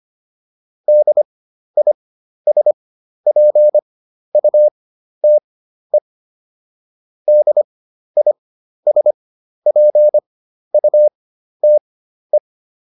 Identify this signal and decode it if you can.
Morse code, heard as a keyed tone, about 25 wpm, spaced as for 12 wpm, 600 Hz. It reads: DISPUTE DISPUTE